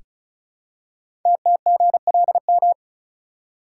Morse code, heard as a keyed tone, about 36 wpm, 700 Hz. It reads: TTGLM
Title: Code Groups - 5 Letters - 35wpm